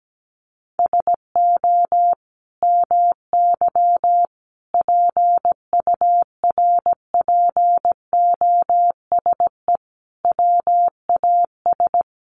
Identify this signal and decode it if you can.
Morse code, heard as a keyed tone, about 17 wpm, 700 Hz. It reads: SO MY PURPOSE WAS
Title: Sentences from Top 1100 Words 17wpm